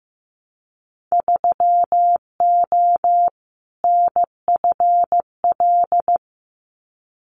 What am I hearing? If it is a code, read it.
Morse code, heard as a keyed tone, about 15 wpm, 700 Hz. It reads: 3O NFL
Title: Winter Field Day Exchanges 15wpm